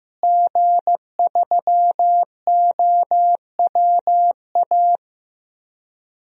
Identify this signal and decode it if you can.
Morse code, heard as a keyed tone, about 15 wpm, 700 Hz. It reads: G3OWA
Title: DXCC-100 Call Signs - 5 Characters - Repeated 3x - 15wpm